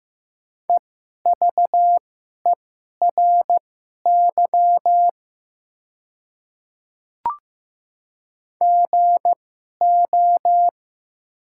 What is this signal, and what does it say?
Morse code, heard as a keyed tone, about 15 wpm, spaced as for 11 wpm, 700 Hz. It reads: EVERY GO
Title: 2x Character Spacing - Top 300 Words - 15wpm